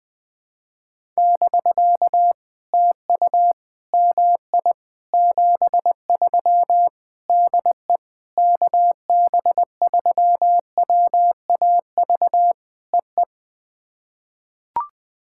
Morse code, heard as a keyed tone, about 20 wpm, 700 Hz. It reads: <BK> TU MI 73 DE KB3WAV EE
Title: POTA QSO Practice - 20wpm